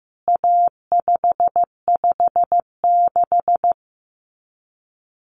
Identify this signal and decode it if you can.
Morse code, heard as a keyed tone, about 14 wpm, 700 Hz. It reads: A556
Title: CWT Exchange 15wpm